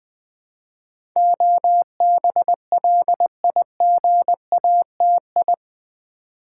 Morse code, heard as a keyed tone, about 20 wpm, 700 Hz. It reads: OBLIGATI
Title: Build A Word - 4 Letters At A Time - 10-Letter Words - 20wpm